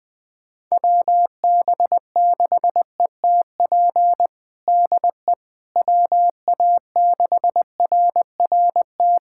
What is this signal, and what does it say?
Morse code, heard as a keyed tone, about 21 wpm, 700 Hz. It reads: WB6ETP DE WA6RRT
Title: USA Callsign DE Callsign - 6 Characters - 20wpm